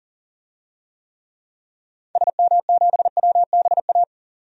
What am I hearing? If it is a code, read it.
Morse code, heard as a keyed tone, about 40 wpm, 700 Hz. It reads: SM7WBA
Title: DXCC-100 Call Signs - 6 Characters - 40wpm